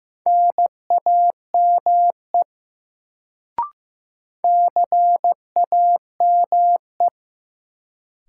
Morse code, heard as a keyed tone, about 15 wpm, 700 Hz. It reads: NAME CAME
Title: Sentences from Top 200 Words - New Word Drill - 15wpm